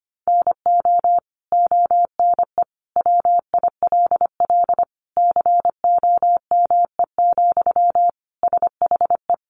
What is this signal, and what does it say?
Morse code, heard as a keyed tone, about 25 wpm, 700 Hz. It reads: NO ONE WILL COME, SHE
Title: Sentences from Top 100 Words 25wpm